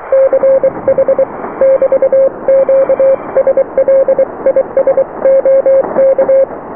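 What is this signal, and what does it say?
Morse code, heard as a keyed tone, about 23 wpm, 540 Hz. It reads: CH <BT> QSL IS OK
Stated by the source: Morse QSO on 14 MHz with a German station, received at RST 579.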